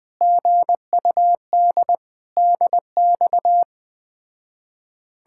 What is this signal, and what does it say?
Morse code, heard as a keyed tone, about 20 wpm, 700 Hz. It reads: GUD DX
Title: Speed-Racing Common QSO Phrases - 17wpm